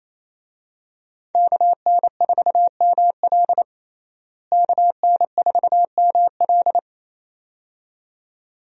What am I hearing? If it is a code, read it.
Morse code, heard as a keyed tone, about 28 wpm, 700 Hz. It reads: KN4ML KN4ML
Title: USA Call Signs - 5 Characters - Repeated 2x - 28wpm